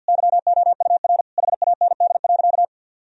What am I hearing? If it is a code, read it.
Morse code, heard as a keyed tone, about 47 wpm, 700 Hz. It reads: YOUR HAND.
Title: Sentences from Top 1000 Words - Review Part 2 - 50wpm